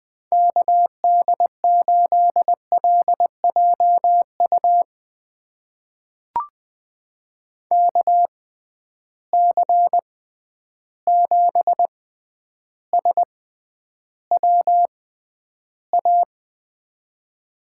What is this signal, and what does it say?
Morse code, heard as a keyed tone, about 21 wpm, spaced as for 10 wpm, 700 Hz. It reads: KD8LJU KC7SWA
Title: USA Call Signs - 6x Character Spacing - 20wpm